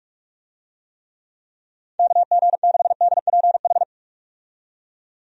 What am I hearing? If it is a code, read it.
Morse code, heard as a keyed tone, about 48 wpm, 700 Hz. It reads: KG6DPH
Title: USA Call Signs - 6 Characters - 45wpm